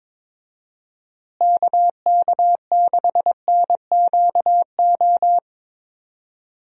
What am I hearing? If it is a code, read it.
Morse code, heard as a keyed tone, about 22 wpm, 700 Hz. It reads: KK6NQO
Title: USA Call Signs - 6 Characters - Repeated 2x - 22wpm